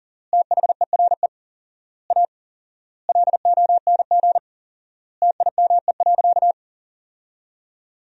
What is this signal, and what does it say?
Morse code, heard as a keyed tone, about 40 wpm, 700 Hz. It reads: THERE A LONG TIME.
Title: 4x Word-Spacing - Sentences from Top 400 Words - 40wpm